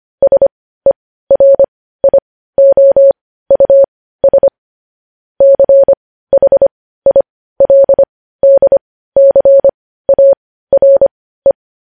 Morse code, heard as a keyed tone, about 25 wpm, spaced as for 15 wpm, 550 Hz. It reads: SERIOUS CHILDCARE
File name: podcast / News headlines-1x-25x15